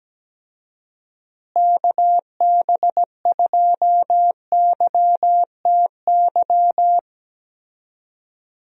Morse code, heard as a keyed tone, about 17 wpm, 700 Hz. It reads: KB2YTY
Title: USA Call Signs - 6 Characters - Repeated 3x - 17wpm